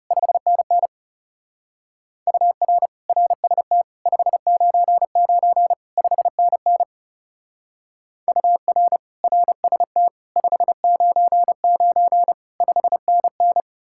Morse code, heard as a keyed tone, about 38 wpm, 700 Hz. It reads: 5NN UR RST 599 5NN UR RST 599 5NN